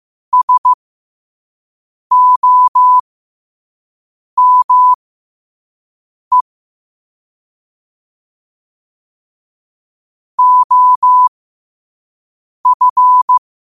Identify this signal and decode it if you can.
Morse code, heard as a keyed tone, about 15 wpm, spaced as for 5 wpm, 1000 Hz. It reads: SOME OF
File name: qotdfiles / 5wpm